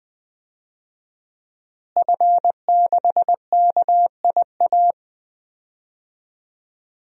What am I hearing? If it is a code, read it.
Morse code, heard as a keyed tone, about 21 wpm, 700 Hz. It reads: F6KIA
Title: DXCC-100 Call Signs - 5 Characters - Repeated 3x - 20wpm